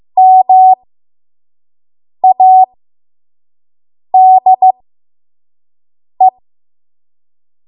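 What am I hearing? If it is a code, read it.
Morse code, heard as a keyed tone, about 15 wpm, spaced as for 5 wpm, 750 Hz. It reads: MADE
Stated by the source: 5 WPM Morse code practice.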